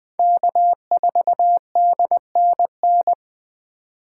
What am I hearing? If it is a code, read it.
Morse code, heard as a keyed tone, about 20 wpm, 700 Hz. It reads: K4DNN
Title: Speed Racing - USA 1x3 Call Signs 17wpm